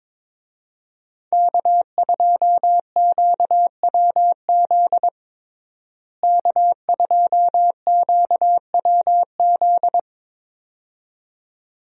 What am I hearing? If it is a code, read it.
Morse code, heard as a keyed tone, about 22 wpm, 700 Hz. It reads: K2QWZ K2QWZ
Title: USA Call Signs - 5 Characters - Repeated 2x - 22wpm